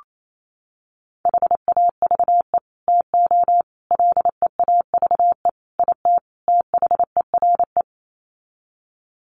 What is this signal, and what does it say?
Morse code, heard as a keyed tone, about 28 wpm, 700 Hz. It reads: HAVE TO LEAVE IT THERE
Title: Sets of 5 Words Encore2 28wpm